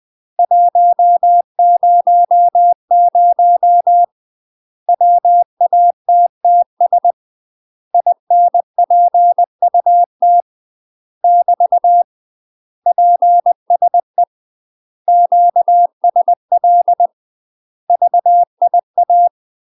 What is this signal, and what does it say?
Morse code, heard as a keyed tone, about 20 wpm, 700 Hz. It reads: 100 WATTS INPUT <BT> PSE QSL VIA
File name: UK MASTER QSOs TRAINING / 20WPM / UK MASTER QSOs 0040